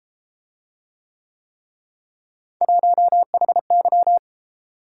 Morse code, heard as a keyed tone, about 30 wpm, 700 Hz. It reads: 1HY